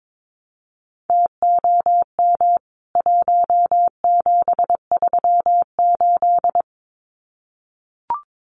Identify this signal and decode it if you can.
Morse code, heard as a keyed tone, about 22 wpm, 700 Hz. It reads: TOM 1738
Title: CWT Exchange 22wpm